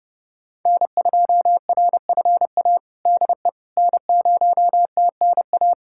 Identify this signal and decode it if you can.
Morse code, heard as a keyed tone, about 30 wpm, 700 Hz. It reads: N2RFA DE N0TNA
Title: USA Callsign DE Callsign - 5 Characters - 30wpm